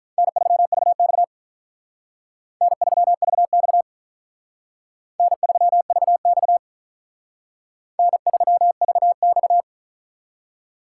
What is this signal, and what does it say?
Morse code, heard as a keyed tone, about 44 wpm, 700 Hz. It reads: N3VX N3VX N3VX N3VX